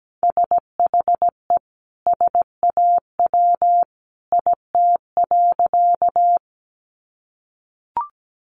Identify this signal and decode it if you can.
Morse code, heard as a keyed tone, about 17 wpm, 700 Hz. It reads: SHE SAW IT.
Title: Sentences from Top 300 Words 17wpm